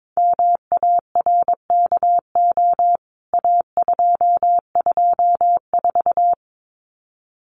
Morse code, heard as a keyed tone, about 22 wpm, 700 Hz. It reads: MARKO A224
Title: CWT Exchange 22wpm